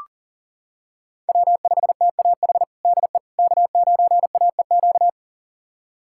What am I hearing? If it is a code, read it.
Morse code, heard as a keyed tone, about 40 wpm, 700 Hz. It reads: W5TAH DE K9AEQ
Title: USA Callsign DE Callsign - 5 Characters - 40wpm